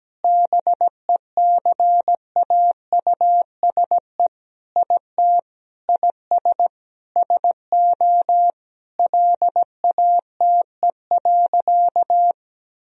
Morse code, heard as a keyed tone, about 17 wpm, 700 Hz. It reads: BECAUSE IT IS SO LATE.
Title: Sentences from Top 1000 Words - Review Part 3 - 17wpm